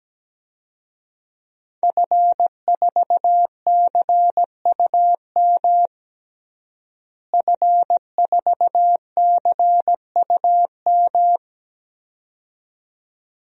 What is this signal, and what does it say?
Morse code, heard as a keyed tone, about 17 wpm, 700 Hz. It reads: F4CUM F4CUM
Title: DXCC-100 Call Signs - 5 Characters - Repeated 2x - 17wpm